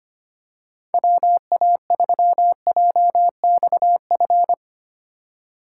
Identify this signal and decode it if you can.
Morse code, heard as a keyed tone, about 25 wpm, 700 Hz. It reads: WA3JXF
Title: USA Call Signs - 6 Characters - 8x Character Spacing - 25wpm